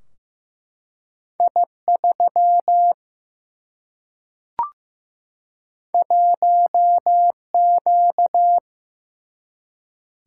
Morse code, heard as a keyed tone, about 15 wpm, 700 Hz. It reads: I 3 1 Q
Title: Single-Letter Single-Number 15wpm